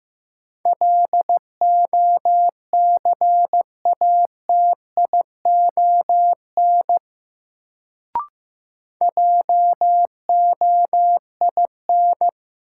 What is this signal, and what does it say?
Morse code, heard as a keyed tone, about 16 wpm, 700 Hz. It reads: LOCATION JOIN
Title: Sentences from Top 1000 Words - Review Part 6 - New Word Drill - 15wpm